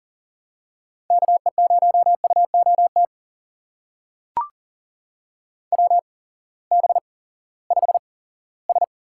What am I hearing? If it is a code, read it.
Morse code, heard as a keyed tone, about 40 wpm, spaced as for 20 wpm, 700 Hz. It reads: KE0UOT WB5S
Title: USA Call Signs - 6 Characters - 8x Character Spacing - 40wpm